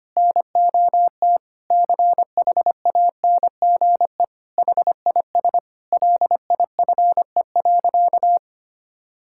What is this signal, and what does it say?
Morse code, heard as a keyed tone, about 25 wpm, 700 Hz. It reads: NOT CHANGE HIS LIFE.
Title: Sentences from Top 1000 Words - Review Part 5 - 25wpm